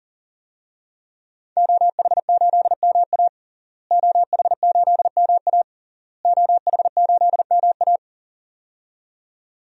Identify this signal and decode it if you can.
Morse code, heard as a keyed tone, about 40 wpm, 700 Hz. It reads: OH8MA OH8MA OH8MA
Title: DXCC-100 Call Signs - 5 Characters - Repeated 3x - 40wpm